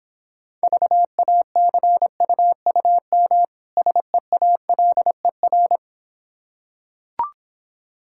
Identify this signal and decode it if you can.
Morse code, heard as a keyed tone, about 26 wpm, 690 Hz. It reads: VACUUM SEALER